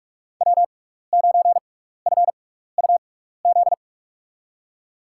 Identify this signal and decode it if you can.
Morse code, heard as a keyed tone, about 42 wpm, spaced as for 15 wpm, 700 Hz. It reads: W9FUZ
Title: USA Call Signs - 5 Characters - 6x Character Spacing - 45wpm